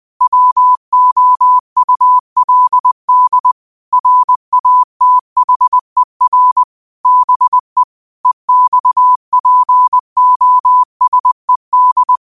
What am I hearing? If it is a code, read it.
Morse code, heard as a keyed tone, about 20 wpm, 1000 Hz. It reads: WOULD RATHER BE EXPOSED